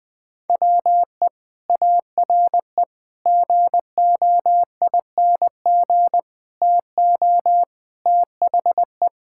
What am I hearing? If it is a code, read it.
Morse code, heard as a keyed tone, about 20 wpm, 700 Hz. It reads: WE ARE GOING TO THE
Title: Sentences with 4-Word Phrases - Spoken Beginning - 20wpm